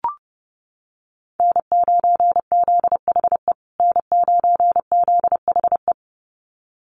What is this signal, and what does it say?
Morse code, heard as a keyed tone, about 30 wpm, 700 Hz. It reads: N9ZHE N9ZHE
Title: USA Call Signs Repeated 2x 30wpm